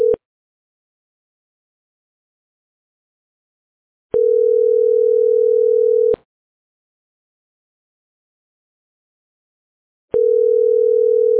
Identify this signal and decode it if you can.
Ringback tone (North America), 440 + 480 Hz.